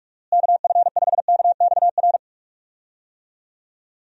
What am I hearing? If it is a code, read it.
Morse code, heard as a keyed tone, about 42 wpm, 700 Hz. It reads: KF5KXR
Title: USA Call Signs - 6 Characters - Repeated 3x - 45wpm